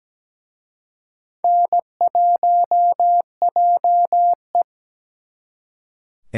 Morse code, heard as a keyed tone, about 20 wpm, 700 Hz. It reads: N1JE
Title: Speed Racing - USA 1x2 Call Signs 17wpm